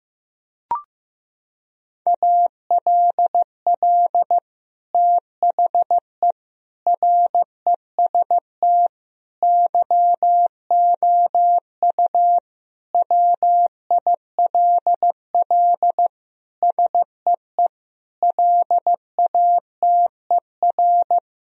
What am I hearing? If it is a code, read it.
Morse code, heard as a keyed tone, about 16 wpm, 700 Hz. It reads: ALL THE REST YOU WILL SEE LATER
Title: Sentences from Top 1000 Words - Review Part 4 - 15wpm